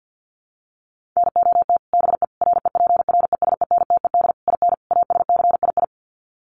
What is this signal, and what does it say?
Morse code, heard as a keyed tone, about 50 wpm, 700 Hz. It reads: NOT BE REPRESENTED IN ASCII
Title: Sets of 5 Words 50wpm